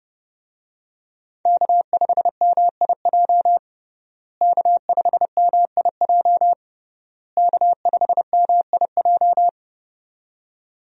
Morse code, heard as a keyed tone, about 31 wpm, 700 Hz. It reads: K5MIJ K5MIJ K5MIJ
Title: USA Call Signs - 5 Characters - Repeated 3x - 30wpm